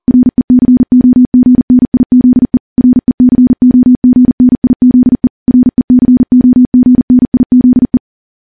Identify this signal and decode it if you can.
Morse code, heard as a keyed tone, about 40 wpm, 250 Hz. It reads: RECOGNIZE RECOGNIZE RECOGNIZE
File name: morse / morse-training-40WPM